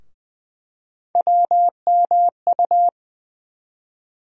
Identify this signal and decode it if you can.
Morse code, heard as a keyed tone, about 20 wpm, 700 Hz. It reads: WMU
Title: USA Call Sign Postfixes - 3 Letters 20wpm